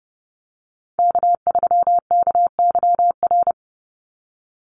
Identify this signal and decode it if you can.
Morse code, heard as a keyed tone, about 30 wpm, 700 Hz. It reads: K3KYR
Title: NA SOTA Chaser Callsigns 30wpm